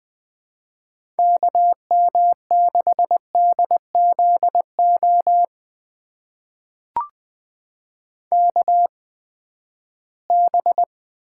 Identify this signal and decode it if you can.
Morse code, heard as a keyed tone, about 20 wpm, spaced as for 14 wpm, 700 Hz. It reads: KM6DZO KB
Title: USA Call Signs - 6 Characters - 8x Character Spacing - 20wpm